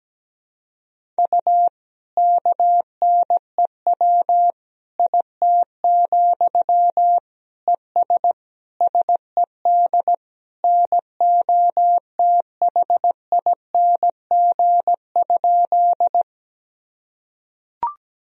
Morse code, heard as a keyed tone, about 17 wpm, 700 Hz. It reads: U KNEW IT, ES SED NOTHING?
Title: Sentences from Top 500 Words and Abbreviations 17wpm